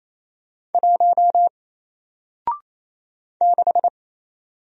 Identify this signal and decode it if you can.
Morse code, heard as a keyed tone, about 28 wpm, 700 Hz. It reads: 1 6
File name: Single-Letter-Number-28wpm